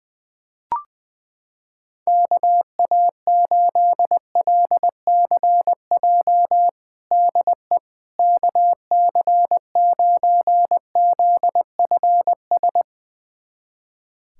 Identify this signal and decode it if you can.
Morse code, heard as a keyed tone, about 21 wpm, 700 Hz. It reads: KA8LCJ DE KC9ZFS
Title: USA Callsign DE Callsign - 6 Characters - 20wpm